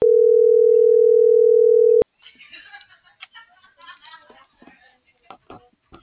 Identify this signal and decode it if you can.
Ringback tone (North America), 440 + 480 Hz.